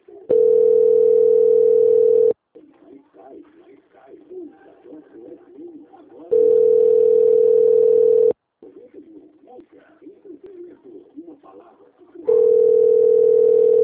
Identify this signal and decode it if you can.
Ringback tone (North America), 440 + 480 Hz.
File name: out- - - - - .5092